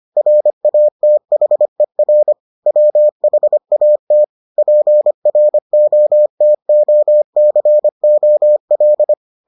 Morse code, heard as a keyed tone, about 25 wpm, 600 Hz. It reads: RATHER WHAT PROTOCOL